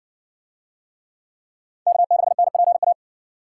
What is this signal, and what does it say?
Morse code, heard as a keyed tone, about 52 wpm, 700 Hz. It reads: K6NPA